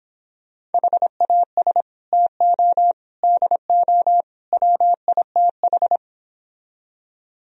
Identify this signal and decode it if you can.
Morse code, heard as a keyed tone, about 29 wpm, 700 Hz. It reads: HAS TO DO WITH